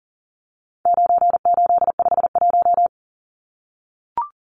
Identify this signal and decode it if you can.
Morse code, heard as a keyed tone, about 38 wpm, 700 Hz. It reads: 9851
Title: Numbers - 4-Digit 40wpm